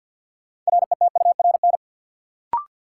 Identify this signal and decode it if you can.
Morse code, heard as a keyed tone, about 51 wpm, 700 Hz. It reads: RETURN
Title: Sentences from Top 800 Words - New Word Drill - 50wpm